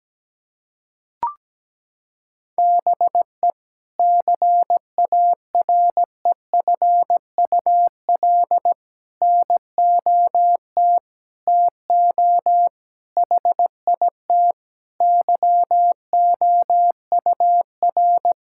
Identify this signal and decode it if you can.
Morse code, heard as a keyed tone, about 17 wpm, 700 Hz. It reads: BE CAREFUL NOT TO HIT YOUR
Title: Sentences from Top 1600 Words 17wpm